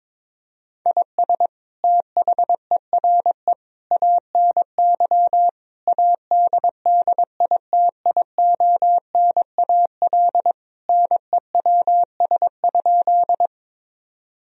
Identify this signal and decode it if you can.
Morse code, heard as a keyed tone, about 21 wpm, 700 Hz. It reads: IS THERE ANY ADDITIONAL NEWS?
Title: Sentences from Top 1000 Words - Review Part 4 - 22wpm